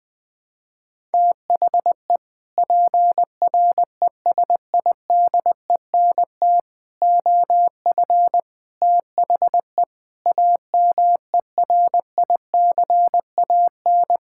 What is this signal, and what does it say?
Morse code, heard as a keyed tone, about 20 wpm, 700 Hz. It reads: THE PRESIDENT OF THE AMERICAN
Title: Sentences with 3-Word Phrases - 20wpm